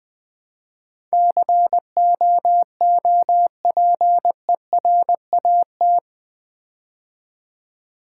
Morse code, heard as a keyed tone, about 20 wpm, 700 Hz. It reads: COOPERAT
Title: Build A Word - 4 Letters At A Time - 11-Letter Words - 20wpm